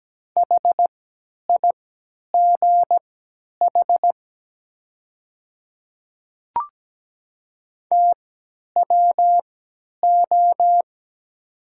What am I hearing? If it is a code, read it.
Morse code, heard as a keyed tone, about 17 wpm, spaced as for 10 wpm, 690 Hz. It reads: HIGH TWO
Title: 3x Character Spacing - Top 300 Words - 17wpm